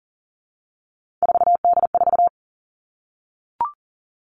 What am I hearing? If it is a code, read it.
Morse code, heard as a keyed tone, about 40 wpm, 700 Hz. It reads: 4D4
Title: All Cut Numbers - 3-Digit 40wpm